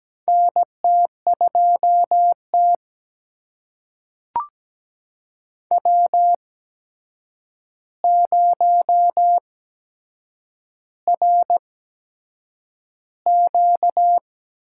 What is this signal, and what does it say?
Morse code, heard as a keyed tone, about 17 wpm, spaced as for 7 wpm, 700 Hz. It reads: NT2T W0RQ
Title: USA Call Signs - 8x Character Spacing - 17wpm